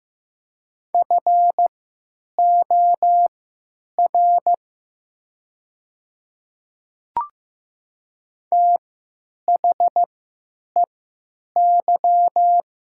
Morse code, heard as a keyed tone, about 15 wpm, spaced as for 9 wpm, 700 Hz. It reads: FOR THEY
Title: 3x Character Spacing - Top 100 Words - 15wpm